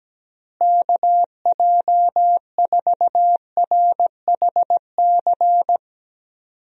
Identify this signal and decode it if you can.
Morse code, heard as a keyed tone, about 17 wpm, 700 Hz. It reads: KJ4RHC
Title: USA Call Signs - 6 Characters - Repeated 2x - 17wpm